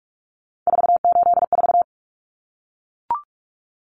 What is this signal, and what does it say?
Morse code, heard as a keyed tone, about 44 wpm, 700 Hz. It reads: 484
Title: Numbers - 3-Digit 45wpm